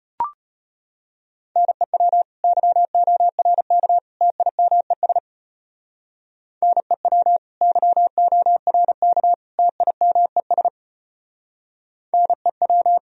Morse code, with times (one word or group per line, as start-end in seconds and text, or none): 1.6-2.2 NEW
2.4-4.0 YORK
4.2-5.2 TIMES
6.6-7.4 NEW
7.6-9.3 YORK
9.6-10.7 TIMES
12.1-13.0 NEW